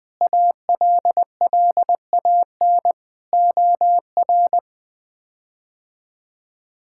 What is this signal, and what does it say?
Morse code, heard as a keyed tone, about 20 wpm, 700 Hz. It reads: ALLAN OR
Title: K1USN SST QSO Practice - 20wpm